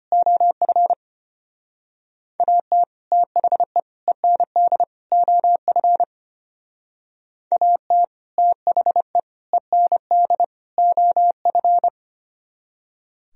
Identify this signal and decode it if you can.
Morse code, heard as a keyed tone, about 31 wpm, 700 Hz. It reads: OF AT THE END OF AT THE END OF
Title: Speed-Racing - Sets of 4 Words 25wpm